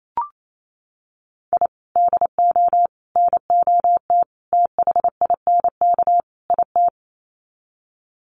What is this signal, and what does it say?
Morse code, heard as a keyed tone, about 28 wpm, 700 Hz. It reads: I DO NOT THINK IT
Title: Sets of 5 Words 28wpm